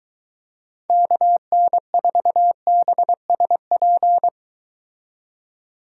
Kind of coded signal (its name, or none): Morse code